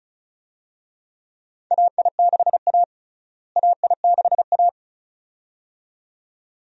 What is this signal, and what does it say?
Morse code, heard as a keyed tone, about 36 wpm, 700 Hz. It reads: AI6A AI6A
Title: USA Call Signs - 4 Characters - Repeated 2x - 35wpm